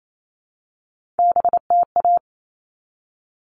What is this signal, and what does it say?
Morse code, heard as a keyed tone, about 28 wpm, 700 Hz. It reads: BTA
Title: All Cut Numbers - 3-Digit 28wpm